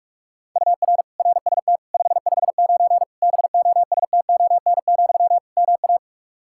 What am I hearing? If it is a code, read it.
Morse code, heard as a keyed tone, about 46 wpm, 700 Hz. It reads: UR RST 559 BOSTON, MA
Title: Practice QSO - Exchange 1 - Beginning - 45wpm